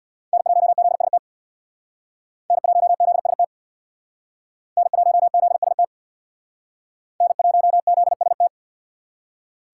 Morse code, heard as a keyed tone, about 64 wpm, 700 Hz. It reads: N1ZST N1ZST N1ZST N1ZST